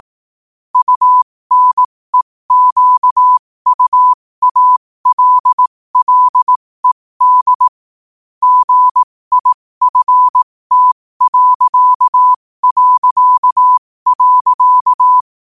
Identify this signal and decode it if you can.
Morse code, heard as a keyed tone, about 18 wpm, spaced as for 15 wpm, 1000 Hz. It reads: UNEQUALLED GIFT...
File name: qotdfiles / 15wpm